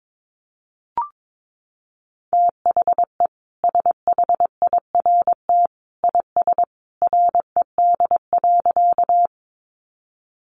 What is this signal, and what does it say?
Morse code, heard as a keyed tone, about 22 wpm, 700 Hz. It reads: THE SHIRT IS RED.
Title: Additional Easy Phrases 22wpm